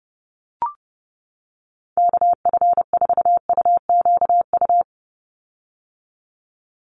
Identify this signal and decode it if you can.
Morse code, heard as a keyed tone, about 30 wpm, 700 Hz. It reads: KF4UQU
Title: USA Call Signs 30wpm